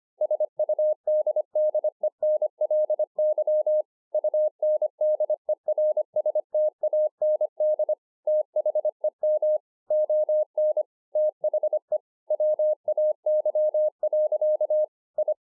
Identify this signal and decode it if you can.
Morse code, heard as a keyed tone, about 25 wpm, 610 Hz. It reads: SUDDENLY UNDERSTAND THEM ON THE WAY. I
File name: 2011-04-mp3 / BEGINNERS DIARY 25WPM